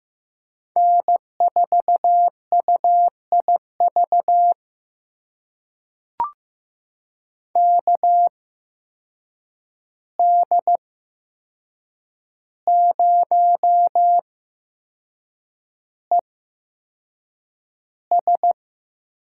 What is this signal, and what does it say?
Morse code, heard as a keyed tone, about 15 wpm, spaced as for 6 wpm, 700 Hz. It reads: N4UIV KD0ES
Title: USA Call Signs - 5 Characters - 8x Character Spacing - 15wpm